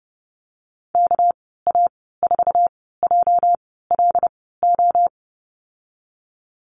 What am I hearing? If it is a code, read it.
Morse code, heard as a keyed tone, about 30 wpm, spaced as for 17 wpm, 700 Hz. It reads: KA4JLO
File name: Lesson 37g - Letter x - Call Signs 30wpm